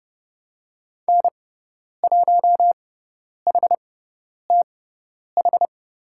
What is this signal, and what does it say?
Morse code, heard as a keyed tone, about 30 wpm, spaced as for 10 wpm, 700 Hz. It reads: N1HTH
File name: Lesson 18g - Letter m - Call Signs 30wpm